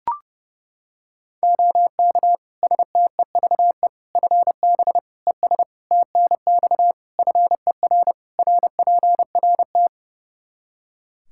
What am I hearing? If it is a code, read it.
Morse code, heard as a keyed tone, about 32 wpm, 700 Hz. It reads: OK STEVE FB ES TNX FER RPRT
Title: Practice QSO - Exchange 2 - Info - 30wpm